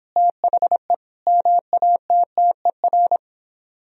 Morse code, heard as a keyed tone, about 29 wpm, 700 Hz. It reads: THE MATTER